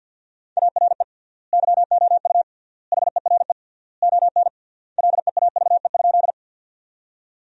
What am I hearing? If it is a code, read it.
Morse code, heard as a keyed tone, about 48 wpm, 700 Hz. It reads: ARE YOU HERE ON LEAVE?
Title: 3x Word-Spacing - Sentences from Top 400 Words - 50wpm